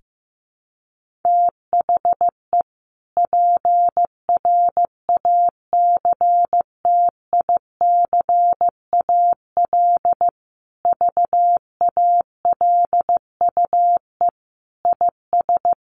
Morse code, heard as a keyed tone, about 15 wpm, 700 Hz. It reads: THE PRACTICAL VALUE IS
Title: Sentences from Top 1400 Words 15wpm